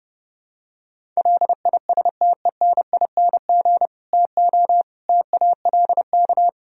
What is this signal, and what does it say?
Morse code, heard as a keyed tone, about 29 wpm, 700 Hz. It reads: LISTENING TO TALK